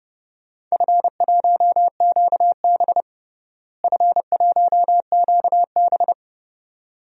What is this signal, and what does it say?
Morse code, heard as a keyed tone, about 30 wpm, 700 Hz. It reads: F1QB F1QB
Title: DXCC-100 Call Signs - 4 Characters - Repeated 2x - 30wpm